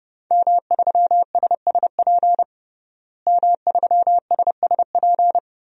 Morse code, heard as a keyed tone, about 32 wpm, 700 Hz. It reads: M3SSP M3SSP
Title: DXCC-100 Call Signs - 5 Characters - Repeated 3x - 30wpm